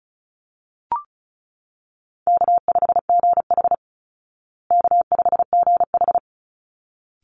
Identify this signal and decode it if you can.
Morse code, heard as a keyed tone, about 34 wpm, 700 Hz. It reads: K5GH K5GH
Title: Easy Call Signs Repeated 2x - 35wpm